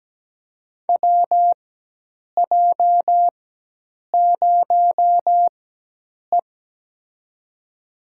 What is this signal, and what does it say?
Morse code, heard as a keyed tone, about 17 wpm, spaced as for 8 wpm, 700 Hz. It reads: WJ0E
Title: USA Call Signs - 4 Characters - 4x Character Spacing - 17wpm